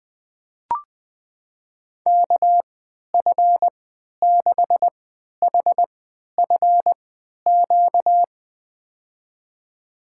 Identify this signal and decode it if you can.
Morse code, heard as a keyed tone, about 20 wpm, spaced as for 11 wpm, 700 Hz. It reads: KF6HFQ
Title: USA Call Signs - 3x Character Spacing - 20wpm